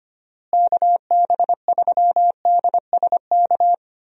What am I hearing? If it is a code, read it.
Morse code, heard as a keyed tone, about 25 wpm, 700 Hz. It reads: KB3DSK
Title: USA Call Signs - 6 Characters - Repeated 2x - 25wpm